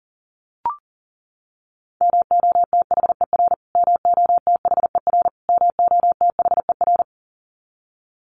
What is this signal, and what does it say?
Morse code, heard as a keyed tone, about 40 wpm, 700 Hz. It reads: MOTHER MOTHER MOTHER
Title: Top 500 Words Repeated 40wpm